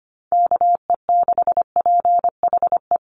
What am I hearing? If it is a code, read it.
Morse code, heard as a keyed tone, about 25 wpm, 700 Hz. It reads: KE6PHE
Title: USA Call Signs Repeated 3x 25wpm